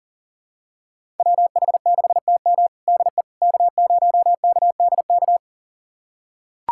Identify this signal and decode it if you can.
Morse code, heard as a keyed tone, about 40 wpm, 700 Hz. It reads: WH6TM DE K0KDK
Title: USA Callsign DE Callsign - 5 Characters - 40wpm